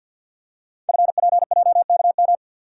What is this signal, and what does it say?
Morse code, heard as a keyed tone, about 47 wpm, 700 Hz. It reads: FPJKM
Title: Code Groups - 5 Letter:Numbers - 50wpm